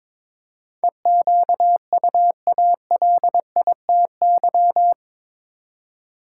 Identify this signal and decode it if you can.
Morse code, heard as a keyed tone, about 22 wpm, 700 Hz. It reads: EQUALITY
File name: practice / 22wpm / Eight Letter Words - 22wpm